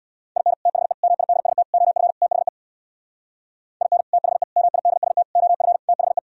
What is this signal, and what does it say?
Morse code, heard as a keyed tone, about 64 wpm, 700 Hz. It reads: IN THE MIDST OF THE IN THE MIDST OF THE